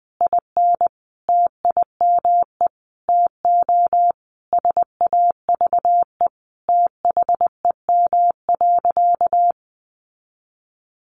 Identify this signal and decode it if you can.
Morse code, heard as a keyed tone, about 20 wpm, 700 Hz. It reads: IN TIME TO SAVE THEM.
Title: Sentences from Top 1100 Words 20wpm